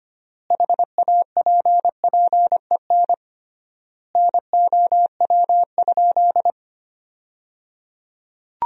Morse code, heard as a keyed tone, about 25 wpm, 700 Hz. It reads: HAPPEN NOW?
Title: 3x Word-Spacing - Sentences from Top 400 Words - 25wpm